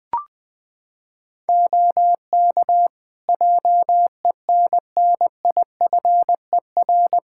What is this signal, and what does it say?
Morse code, heard as a keyed tone, about 21 wpm, 700 Hz. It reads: OK JENNIFER
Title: Practice QSO - Exchange 3 - More Info - 20wpm